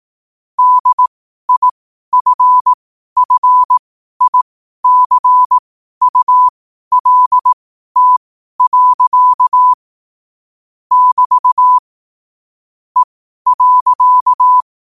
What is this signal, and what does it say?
Morse code, heard as a keyed tone, about 18 wpm, spaced as for 12 wpm, 1000 Hz. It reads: DIFFICULT. <BT> E.
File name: qotdfiles / 12wpm